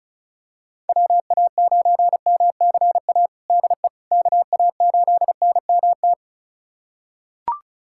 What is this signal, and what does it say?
Morse code, heard as a keyed tone, about 36 wpm, 700 Hz. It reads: WA9MCA DE KA8NMT
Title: USA Callsign DE Callsign - 6 Characters - 35wpm